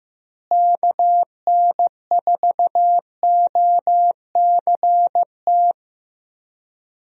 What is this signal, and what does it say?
Morse code, heard as a keyed tone, about 15 wpm, 700 Hz. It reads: KN4OCT
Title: USA Call Signs - 6 Characters - Repeated 3x - 15wpm